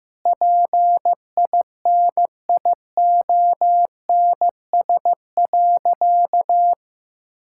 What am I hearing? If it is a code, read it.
Morse code, heard as a keyed tone, about 16 wpm, 700 Hz. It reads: PINIONS.
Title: Pangram Challenge v2 - 15wpm